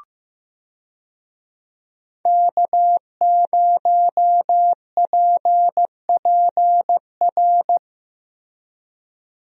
Morse code, heard as a keyed tone, about 15 wpm, 700 Hz. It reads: K0PPR
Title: USA Call Signs - 5 Characters - Repeated 3x - 15wpm